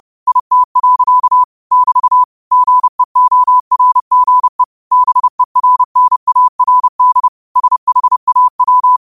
Morse code, heard as a keyed tone, about 30 wpm, 1000 Hz. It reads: IT. <BT> GEORGE BERNARD SHAW